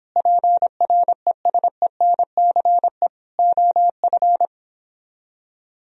Morse code, heard as a keyed tone, about 31 wpm, 700 Hz. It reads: PRESENCE OF